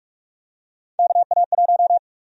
Morse code, heard as a keyed tone, about 47 wpm, 700 Hz. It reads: KA1
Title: USA Call Sign Prefixes - 45wpm